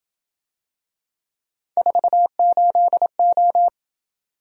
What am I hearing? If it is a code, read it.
Morse code, heard as a keyed tone, about 26 wpm, 700 Hz. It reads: 48O